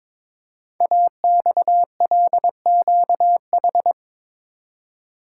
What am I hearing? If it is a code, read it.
Morse code, heard as a keyed tone, about 22 wpm, 700 Hz. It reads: AXLQH
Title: Code Groups - 5 Letters - 22wpm